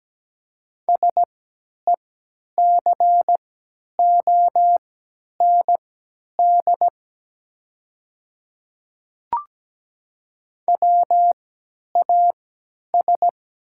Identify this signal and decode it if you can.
Morse code, heard as a keyed tone, about 17 wpm, spaced as for 10 wpm, 700 Hz. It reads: SECOND WAS
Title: 3x Character Spacing - Top 300 Words - 17wpm